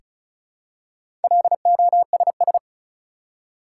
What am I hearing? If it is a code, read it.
Morse code, heard as a keyed tone, about 34 wpm, 700 Hz. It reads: LOSS
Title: Four Letter Words - 35wpm